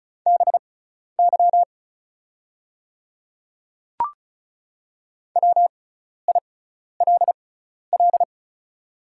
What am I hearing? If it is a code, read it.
Morse code, heard as a keyed tone, about 34 wpm, spaced as for 12 wpm, 700 Hz. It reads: BY WILL
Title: 6x Character Spacing - Top 100 Words - 35wpm